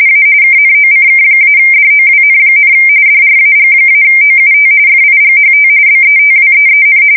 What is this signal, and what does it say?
RTTY, about 45.45 baud. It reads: A KNOCK-DOWN, DRAG-OUT FIGHT BETWEEN
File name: fsk170 / battleshipnj